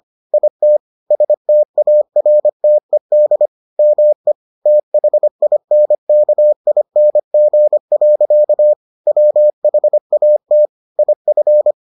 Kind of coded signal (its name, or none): Morse code